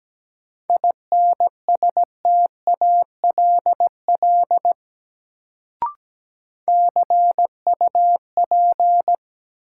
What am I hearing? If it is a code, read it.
Morse code, heard as a keyed tone, about 17 wpm, 700 Hz. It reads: INSTALL CUP
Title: Sentences from Top 1600 Words - New Word Drill - 17wpm